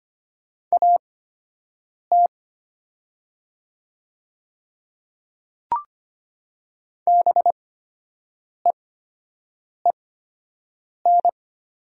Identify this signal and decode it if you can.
Morse code, heard as a keyed tone, about 25 wpm, spaced as for 7 wpm, 700 Hz. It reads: AT BEEN
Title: 8x Character Spacing - Top 100 Words - 25wpm